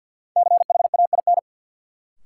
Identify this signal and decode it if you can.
Morse code, heard as a keyed tone, about 53 wpm, 700 Hz. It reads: CHAIN
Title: Sentences from Top 1500 Words - New Word Drill Alt - 50wpm